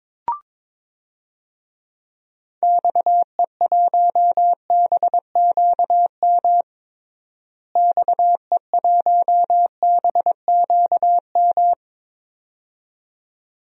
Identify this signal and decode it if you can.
Morse code, heard as a keyed tone, about 22 wpm, 700 Hz. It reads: XE1BQM XE1BQM
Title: DXCC-100 Call Signs - 6 Characters - Repeated 2x - 22wpm